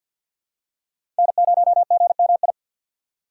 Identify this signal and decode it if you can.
Morse code, heard as a keyed tone, about 48 wpm, 690 Hz. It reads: N0GMI
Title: USA Call Signs - 2x Character Spacing - 50wpm